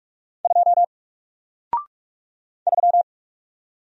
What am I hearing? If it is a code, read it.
Morse code, heard as a keyed tone, about 47 wpm, 700 Hz. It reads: 2 3
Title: Single Letter-Number Alt 45wpm